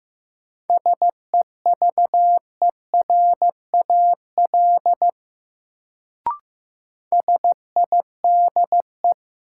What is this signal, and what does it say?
Morse code, heard as a keyed tone, about 15 wpm, 700 Hz. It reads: SEVERAL SIDE
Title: Sentences from Top 400 Words - New Word Drill - 15wpm